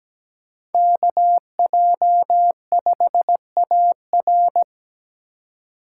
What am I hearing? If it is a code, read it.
Morse code, heard as a keyed tone, about 17 wpm, 700 Hz. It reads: KJ5AR
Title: USA Call Signs - 5 Characters - 2x Character Spacing - 17wpm